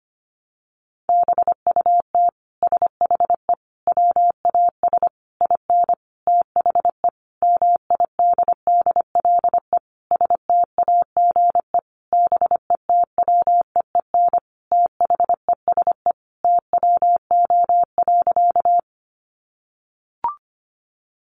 Morse code, heard as a keyed tone, about 25 wpm, 700 Hz. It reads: BUT SHE WAS IN THE MIDDLE STAGE BETWEEN THESE TWO.
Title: Sentences from Top 1200 Words 25wpm